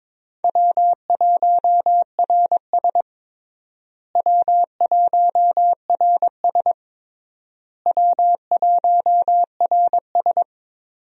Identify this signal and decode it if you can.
Morse code, heard as a keyed tone, about 22 wpm, 700 Hz. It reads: W1RS W1RS W1RS
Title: USA Call Signs - 4 Characters - Repeated 3x - 22wpm